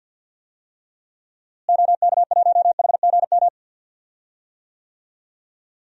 Morse code, heard as a keyed tone, about 49 wpm, 700 Hz. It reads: OK1HGM
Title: DXCC-100 Call Signs - 6 Characters - Repeated 2x - 50wpm